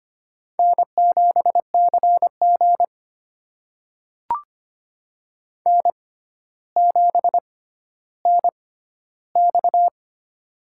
Morse code, heard as a keyed tone, about 25 wpm, spaced as for 13 wpm, 700 Hz. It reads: N7CG N7NX
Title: USA Call Signs - 6x Character Spacing - 25wpm